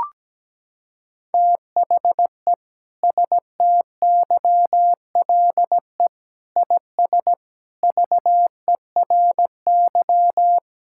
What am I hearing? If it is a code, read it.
Morse code, heard as a keyed tone, about 17 wpm, 700 Hz. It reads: THE STYLE IS VERY
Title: Sentences from Top 1000 Words - Review Part 5 - 17wpm